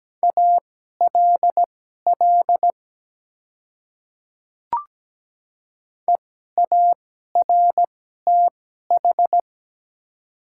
Morse code, heard as a keyed tone, about 17 wpm, spaced as for 12 wpm, 700 Hz. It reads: ALL EARTH
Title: 2x Character Spacing - Top 200 Words - 17wpm